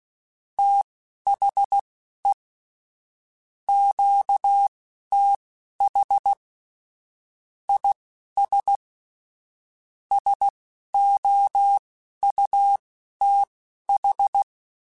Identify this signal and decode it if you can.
Morse code, heard as a keyed tone, about 16 wpm, spaced as for 11 wpm, 800 Hz. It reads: THE QTH IS SOUTH
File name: QSOs / QSOs-mp3s / kc7sac@10